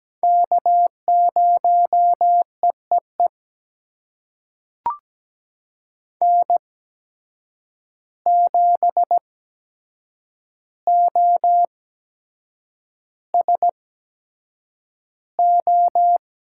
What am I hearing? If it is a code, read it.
Morse code, heard as a keyed tone, about 17 wpm, spaced as for 7 wpm, 700 Hz. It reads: K0EEE N7OSO
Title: USA Call Signs - 5 Characters - 8x Character Spacing - 17wpm